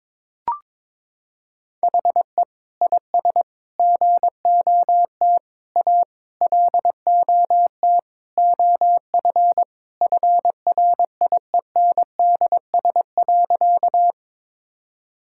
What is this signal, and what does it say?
Morse code, heard as a keyed tone, about 22 wpm, 700 Hz. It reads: HE IS GOT A LOT OF FRIENDS.
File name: practice / 22wpm / Sentences with 4-Word Phrases - 22wpm